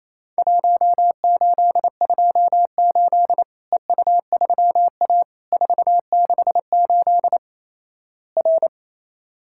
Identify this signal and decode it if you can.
Morse code, heard as a keyed tone, about 27 wpm, 700 Hz. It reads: 1828 EU3A 468 R
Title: Worked All Europe DX Contest - QTC Exchange - 28wpm